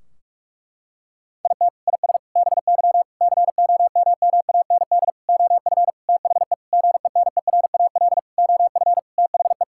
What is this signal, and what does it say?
Morse code, heard as a keyed tone, about 46 wpm, 700 Hz. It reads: IT IS BY COMMAND OF THE GENERAL OF THE
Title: Sentences from Top 1600 Words 45wpm